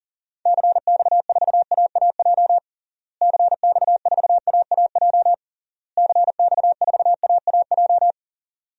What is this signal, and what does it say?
Morse code, heard as a keyed tone, about 40 wpm, 700 Hz. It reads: CX4AAJ CX4AAJ CX4AAJ
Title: DXCC-100 Call Signs - 6 Characters - Repeated 3x - 40wpm